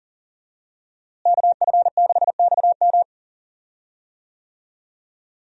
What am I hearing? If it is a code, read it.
Morse code, heard as a keyed tone, about 40 wpm, 700 Hz. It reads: KF6XM
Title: USA Call Signs - 5 Characters - Repeated 3x - 40wpm